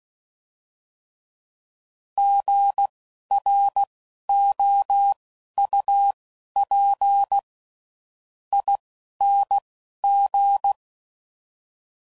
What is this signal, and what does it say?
Morse code, heard as a keyed tone, about 16 wpm, spaced as for 11 wpm, 800 Hz. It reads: GROUP ING
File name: Prefixes / By First Letter